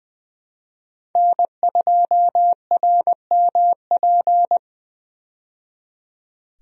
Morse code, heard as a keyed tone, about 20 wpm, 700 Hz. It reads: N2RMP
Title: USA Call Signs - 5 Characters - 20wpm